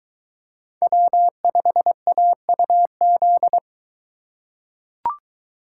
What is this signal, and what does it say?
Morse code, heard as a keyed tone, about 22 wpm, 700 Hz. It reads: W5AUZ